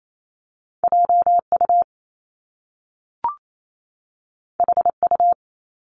Morse code, heard as a keyed tone, about 28 wpm, 700 Hz. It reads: JU HU
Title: Top Two Letter Combinations in Words 28wpm